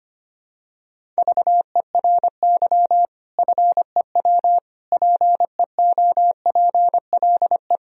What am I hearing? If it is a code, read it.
Morse code, heard as a keyed tone, about 25 wpm, 700 Hz. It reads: VERY FEW PEOPLE
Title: Sentences from Top 1000 Words - Review Part 1 - 25wpm